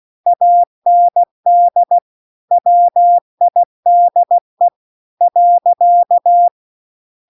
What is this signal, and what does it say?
Morse code, heard as a keyed tone, about 16 wpm, 700 Hz. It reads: AND WIDE .
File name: CW-The-Big-Twoheaded-River-2-16WPM